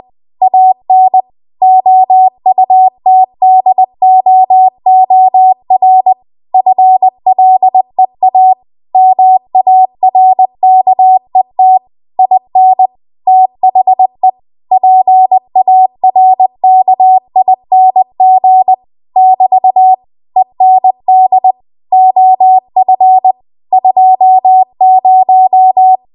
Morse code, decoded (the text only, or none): AN OUTDOOR FLEA MARKET IN THE PARKING <BT> END OF 20